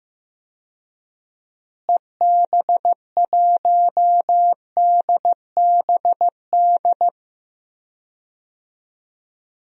Morse code, heard as a keyed tone, about 15 wpm, 700 Hz. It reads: EB1DBD
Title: DXCC-100 Call Signs - 6 Characters - Repeated 3x - 15wpm